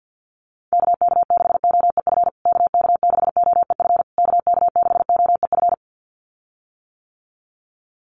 Morse code, heard as a keyed tone, about 50 wpm, 700 Hz. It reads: KK6OEF KK6OEF KK6OEF
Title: USA Call Signs Repeated 3x 50wpm